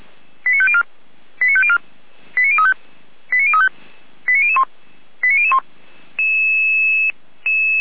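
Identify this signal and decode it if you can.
Five-tone selective call (ZVEI), heard as quick runs of five tones, tones of about 70 ms. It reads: 89504 89504 89035 89035 89001 89001
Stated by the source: ZVEI five-tone call 89099 received by SDR.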